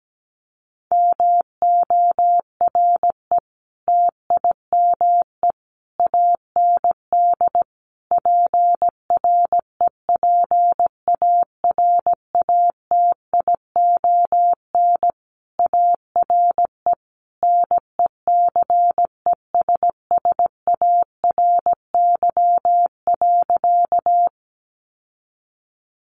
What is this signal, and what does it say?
Morse code, heard as a keyed tone, about 17 wpm, 700 Hz. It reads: MORE TIME AND PREPARATION ARE NECESSARY.
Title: Sentences from Top 1400 Words 17wpm